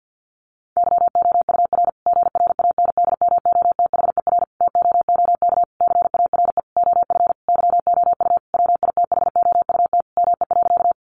Morse code, heard as a keyed tone, about 50 wpm, 700 Hz. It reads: YOUR GRANDMOTHER TOOK CARE OF YOU WITHOUT ME.
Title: Sentences from Top 1200 Words 50wpm